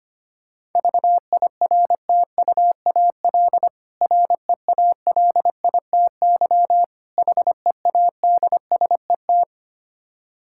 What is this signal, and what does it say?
Morse code, heard as a keyed tone, about 25 wpm, 700 Hz. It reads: VIRTUAL REALITY HEADSET
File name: practice / 25wpm / Difficult Words with Hints - 25wpm